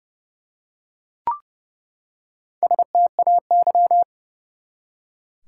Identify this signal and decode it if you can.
Morse code, heard as a keyed tone, about 30 wpm, 700 Hz. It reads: STAY
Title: Four Letter Words - 30wpm